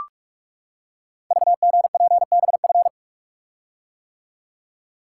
Morse code, heard as a keyed tone, about 44 wpm, 690 Hz. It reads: VGPBF
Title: Code Groups - 5 Letters - 45wpm